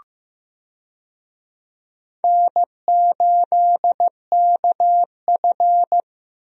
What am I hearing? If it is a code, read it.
Morse code, heard as a keyed tone, about 15 wpm, 700 Hz. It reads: N8KF
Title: USA Call Signs - 4 Characters - Repeated 2x - 15wpm